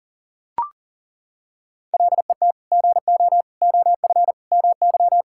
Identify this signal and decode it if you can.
Morse code, heard as a keyed tone, about 40 wpm, 700 Hz. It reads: LET GO OF MY
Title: Sentences from Top 1600 Words 40wpm